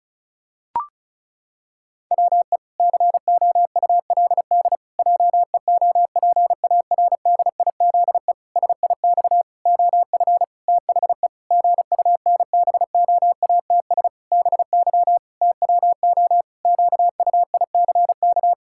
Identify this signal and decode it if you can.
Morse code, heard as a keyed tone, about 36 wpm, 700 Hz. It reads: WE COULD JEOPARDIZE SIX OF THE GUNBOATS BY TWO QUICK
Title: Pangram Challenge v2 - 35wpm